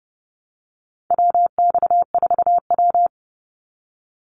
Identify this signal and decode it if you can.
Morse code, heard as a keyed tone, about 30 wpm, 700 Hz. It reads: WX4W
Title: Call Signs - 30wpm